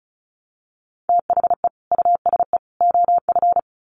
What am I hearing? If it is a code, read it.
Morse code, heard as a keyed tone, about 34 wpm, 700 Hz. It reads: THE USE OF
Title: Sets of 3 Words 35wpm